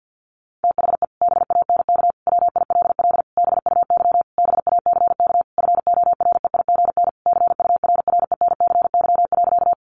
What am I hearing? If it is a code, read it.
Morse code, heard as a keyed tone, about 50 wpm, 700 Hz. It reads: THE BANK WILL BUY BACK FOREIGN CURRENCY.
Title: Sentences from Top 1500 Words 50wpm